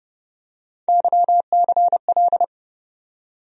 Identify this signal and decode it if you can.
Morse code, heard as a keyed tone, about 31 wpm, 700 Hz. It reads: YCL
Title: USA Call Sign Postfixes - 3 Letters 30wpm